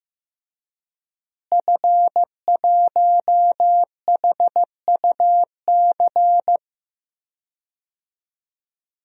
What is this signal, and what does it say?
Morse code, heard as a keyed tone, about 16 wpm, 700 Hz. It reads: F1HUC
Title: DXCC-100 Call Signs - 5 Characters - Repeated 3x - 15wpm